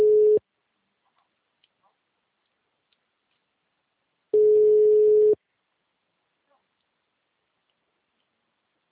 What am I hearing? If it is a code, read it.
Ringback tone (Europe, 425 Hz).